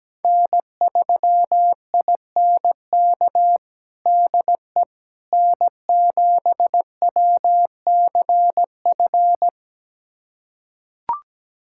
Morse code, heard as a keyed tone, about 17 wpm, 700 Hz. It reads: N3INK DE N7WCF
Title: USA Callsign DE Callsign - 5 Characters - 17wpm